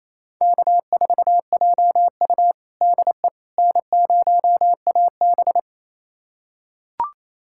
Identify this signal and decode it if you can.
Morse code, heard as a keyed tone, about 28 wpm, 700 Hz. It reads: K4JU DE N0AB
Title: USA Callsign DE Callsign - 4 Characters - 28wpm